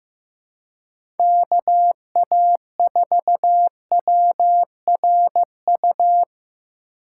Morse code, heard as a keyed tone, about 15 wpm, 700 Hz. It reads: KA4WRU
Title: USA Call Signs - 6 Characters - 2x Character Spacing - 15wpm